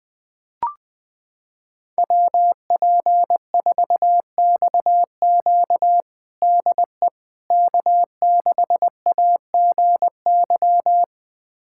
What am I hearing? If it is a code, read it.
Morse code, heard as a keyed tone, about 21 wpm, 700 Hz. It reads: WP4XQ DE K6AGY
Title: USA Callsign DE Callsign 20wpm